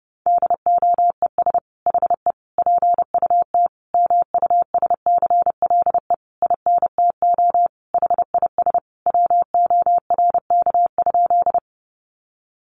Morse code, heard as a keyed tone, about 30 wpm, 700 Hz. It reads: DOES HE PUT MUSCLE INTO HIS WORK?
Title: Sentences from Top 1300 Words 30wpm